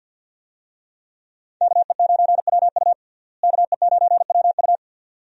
Morse code, heard as a keyed tone, about 53 wpm, 700 Hz. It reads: KE9WU KE9WU
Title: USA Call Signs - 5 Characters - Repeated 2x - 50wpm